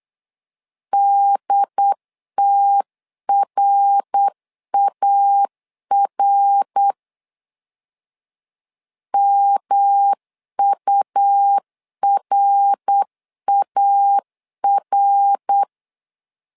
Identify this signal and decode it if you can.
Morse code, heard as a keyed tone, about 8 wpm, 800 Hz. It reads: DTRAR MURAR
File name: LEZIONI / 3 MWRO / 3 MWR 8wpm 2444